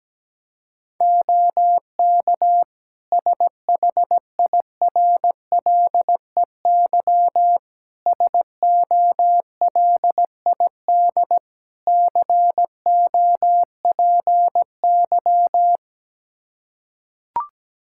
Morse code, heard as a keyed tone, about 17 wpm, 700 Hz. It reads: OK SHIRLEY SOLID COPY
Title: Practice QSO - Exchange 3 - More Info - 17wpm